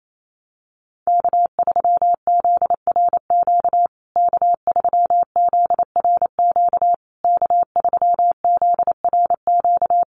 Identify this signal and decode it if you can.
Morse code, heard as a keyed tone, about 28 wpm, 700 Hz. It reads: K3ZRQ K3ZRQ K3ZRQ
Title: USA Call Signs Repeated 3x 28wpm